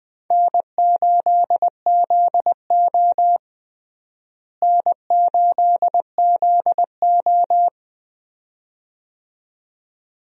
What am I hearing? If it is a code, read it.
Morse code, heard as a keyed tone, about 20 wpm, 700 Hz. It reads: N8ZO N8ZO
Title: USA Call Signs - 4 Characters - Repeated 2x - 20wpm